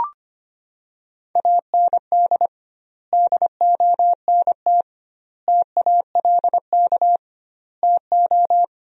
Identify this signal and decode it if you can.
Morse code, heard as a keyed tone, about 25 wpm, 700 Hz. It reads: AND DONT TALK TO
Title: 2x Word-Spacing - Sentences from Top 400 Words - 25wpm